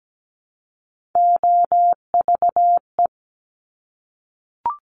Morse code, heard as a keyed tone, about 17 wpm, 700 Hz. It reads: OVE
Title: Top Three Letter Combinations in Words 17wpm